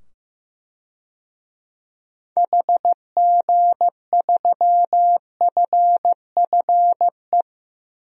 Morse code, heard as a keyed tone, about 15 wpm, 700 Hz. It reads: HG3FFE
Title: DXCC-100 Call Signs - 6 Characters - 15wpm